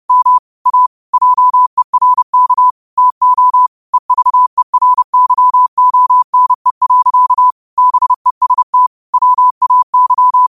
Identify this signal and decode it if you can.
Morse code, heard as a keyed tone, about 30 wpm, 1000 Hz. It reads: M A JERK TO EVERYONE. BEST WAY
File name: qotdfiles / 30wpm